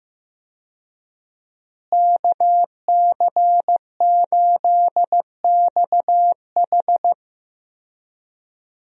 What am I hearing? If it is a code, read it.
Morse code, heard as a keyed tone, about 15 wpm, 700 Hz. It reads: KC8XH
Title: USA Call Signs - 5 Characters - Repeated 3x - 15wpm